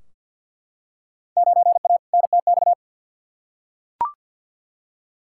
Morse code, heard as a keyed tone, about 49 wpm, 700 Hz. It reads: 9A NTX
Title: Field Day Exchanges Encore - 50wpm